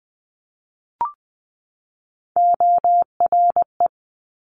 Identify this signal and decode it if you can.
Morse code, heard as a keyed tone, about 20 wpm, 700 Hz. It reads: ORE
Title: Multi-letter Phonemes 20wpm